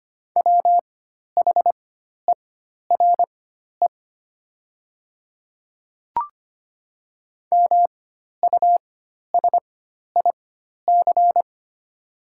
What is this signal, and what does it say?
Morse code, heard as a keyed tone, about 25 wpm, spaced as for 12 wpm, 700 Hz. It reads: WHERE MUSIC
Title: 4x Character Spacing - Top 300 Words - 25wpm